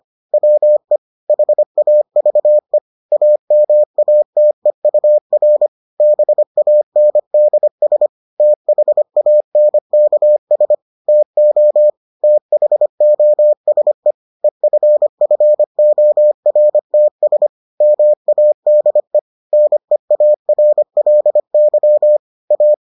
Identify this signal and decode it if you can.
Morse code, heard as a keyed tone, about 25 wpm, 600 Hz. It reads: WE HAVE AMATEUR BANDS THANKS TO THOSE EFFORTS MADE NEARLY A